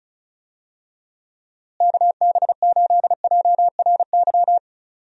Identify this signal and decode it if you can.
Morse code, heard as a keyed tone, about 36 wpm, 700 Hz. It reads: KB8JRY
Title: USA Call Signs - 6 Characters - Repeated 2x - 35wpm